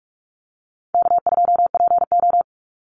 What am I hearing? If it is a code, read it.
Morse code, heard as a keyed tone, about 44 wpm, 700 Hz. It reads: K2PO
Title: Call Signs Repeated 2x - 45wpm